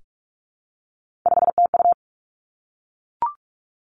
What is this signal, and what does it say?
Morse code, heard as a keyed tone, about 46 wpm, 700 Hz. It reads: 5TU
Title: Three Challenging Letter-Number Combinations 45wpm